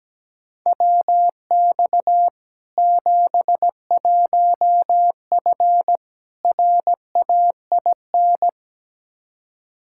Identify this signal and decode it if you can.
Morse code, heard as a keyed tone, about 17 wpm, 700 Hz. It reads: WX 71F RAIN
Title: K1USN SST QSO Practice 17wpm